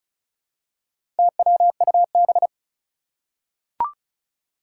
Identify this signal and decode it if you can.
Morse code, heard as a keyed tone, about 36 wpm, 690 Hz. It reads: TWUB
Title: Four Challenging Letter-Number Combinations 35wpm